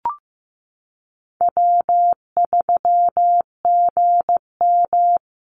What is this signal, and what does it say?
Morse code, heard as a keyed tone, about 15 wpm, 700 Hz. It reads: W3GM
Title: USA Call Signs 15wpm